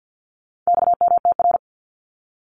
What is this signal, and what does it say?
Morse code, heard as a keyed tone, about 50 wpm, 700 Hz. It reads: XMTR
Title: Abbreviations 50wpm